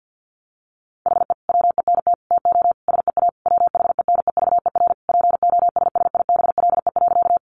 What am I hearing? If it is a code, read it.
Morse code, heard as a keyed tone, about 50 wpm, 700 Hz. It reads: HE WENT TO SEA WHENEVER POSSIBLE.
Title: Sentences from Top 1200 Words 50wpm